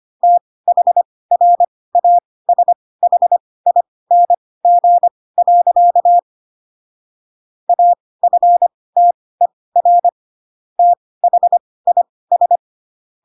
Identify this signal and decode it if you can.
Morse code, heard as a keyed tone, about 25 wpm, spaced as for 18 wpm, 700 Hz. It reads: THRASHING. AFTER THIS
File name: Pinocchio / Pinocchio ch2 12 25 ik1hge